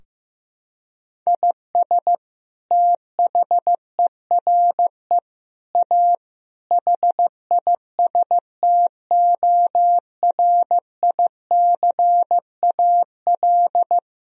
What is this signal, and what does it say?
Morse code, heard as a keyed tone, about 15 wpm, 700 Hz. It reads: IS THERE A HISTORICAL
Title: Sentences from Top 1000 Words - Review Part 4 - 15wpm